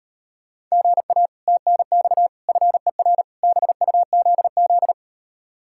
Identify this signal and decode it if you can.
Morse code, heard as a keyed tone, about 38 wpm, 700 Hz. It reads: GA TNX FER BUZZ